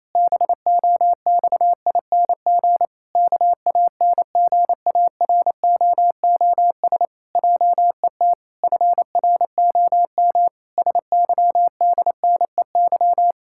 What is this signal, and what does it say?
Morse code, heard as a keyed tone, about 28 wpm, 700 Hz. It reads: BOXING KANGAROOS JET FROM SYDNEY
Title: Pangram Challenge v2 - 28wpm